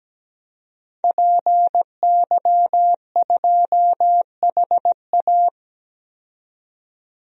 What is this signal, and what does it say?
Morse code, heard as a keyed tone, about 17 wpm, 700 Hz. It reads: PY2HA
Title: DXCC-100 Call Signs - 5 Characters - 17wpm